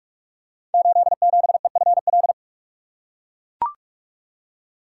Morse code, heard as a keyed tone, about 44 wpm, 700 Hz. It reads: 87EFL
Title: Code Groups - 5 Letter:Numbers - 45wpm